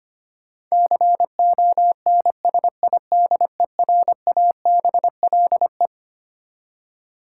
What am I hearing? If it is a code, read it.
Morse code, heard as a keyed tone, about 25 wpm, 700 Hz. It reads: CONSIDERABLE
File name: practice / 25wpm / Twelve Letter Words - 25wpm